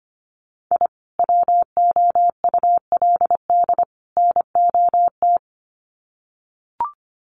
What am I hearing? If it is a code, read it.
Morse code, heard as a keyed tone, about 25 wpm, 700 Hz. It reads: I WOULD NOT
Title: Sets of 3 Words 25wpm